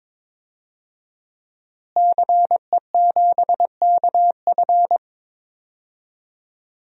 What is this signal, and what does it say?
Morse code, heard as a keyed tone, about 22 wpm, 700 Hz. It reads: CE7KF
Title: DXCC-100 Call Signs - 5 Characters - Repeated 3x - 22wpm